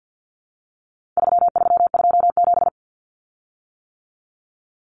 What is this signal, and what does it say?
Morse code, heard as a keyed tone, about 50 wpm, 700 Hz. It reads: 3327
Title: Common Cut Numbers - 4-Digit 50wpm